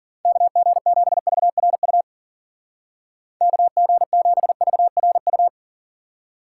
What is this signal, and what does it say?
Morse code, heard as a keyed tone, about 50 wpm, 700 Hz. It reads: KG7VRU KG7VRU
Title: Speed Racing - USA 2x3 Call Signs 40wpm